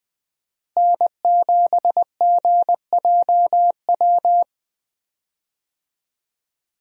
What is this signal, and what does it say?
Morse code, heard as a keyed tone, about 20 wpm, 700 Hz. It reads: N7GJW
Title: USA Call Signs - 5 Characters - Repeated 2x - 20wpm